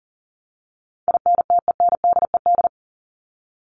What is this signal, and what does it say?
Morse code, heard as a keyed tone, about 40 wpm, 700 Hz. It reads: INTENDED
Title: ED Suffix - Top 100 Words with Base Words 40wpm